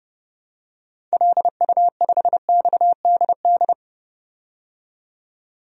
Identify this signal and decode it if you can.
Morse code, heard as a keyed tone, about 32 wpm, 700 Hz. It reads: LU5XDD
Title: DXCC-100 Call Signs - 6 Characters - 30wpm